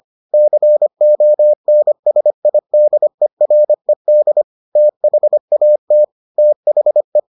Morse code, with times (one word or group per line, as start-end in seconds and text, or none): 0.3-4.4 CONSIDERED
4.8-6.0 THAT
6.4-7.2 THE